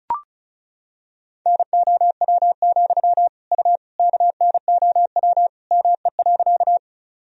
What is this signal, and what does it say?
Morse code, heard as a keyed tone, about 36 wpm, 700 Hz. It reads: NOW, U KNOW ME.
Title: Sentences from Top 200 Words and Abbreviations 35wpm